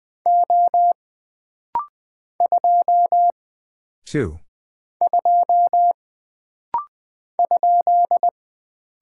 Morse code, heard as a keyed tone, about 20 wpm, 700 Hz. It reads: O 2 2 ?
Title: Single Character (Letter:Number:Punctuation:Prosign) - Rapid-Fire - 20wpm